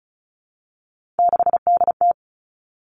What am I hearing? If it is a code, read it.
Morse code, heard as a keyed tone, about 36 wpm, 700 Hz. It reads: 6DT
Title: Three Challenging Letter-Number Combinations 35wpm